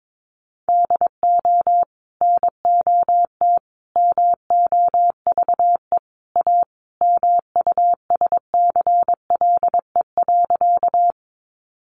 Morse code, heard as a keyed tone, about 22 wpm, 700 Hz. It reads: DO NOT MOVE A MUSCLE.
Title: Sentences from Top 1300 Words 22wpm